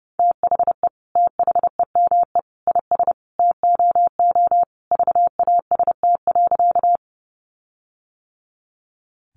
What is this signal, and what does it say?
Morse code, heard as a keyed tone, about 30 wpm, 700 Hz. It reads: THE THEME IS TOO VAST.
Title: Sentences from Top 1300 Words 30wpm